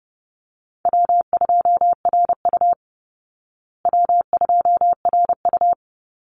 Morse code, heard as a keyed tone, about 30 wpm, 700 Hz. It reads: W2RU W2RU
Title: Easy Call Signs Repeated 3x - 30wpm